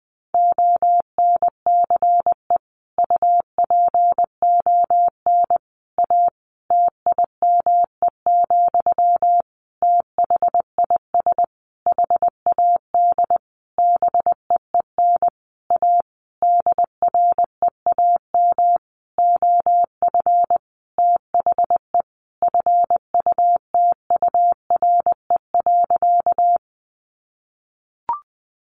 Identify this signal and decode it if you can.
Morse code, heard as a keyed tone, about 20 wpm, 700 Hz. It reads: ONCE UPON A TIME, THIS HAD BEEN A DREAM OF THE FUTURE.
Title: Sentences from Top 1500 Words 20wpm